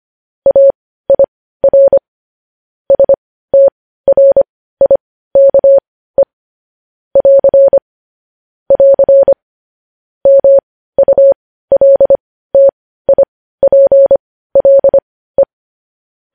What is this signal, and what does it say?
Morse code, heard as a keyed tone, about 25 wpm, spaced as for 15 wpm, 550 Hz. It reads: AIR STRIKE <AR> <AR> MULTIPLE
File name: podcast / News headlines-1x-25x15